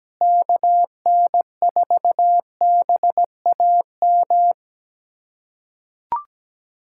Morse code, heard as a keyed tone, about 17 wpm, 700 Hz. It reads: KN4BAM
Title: USA Callsign DE Callsign 17wpm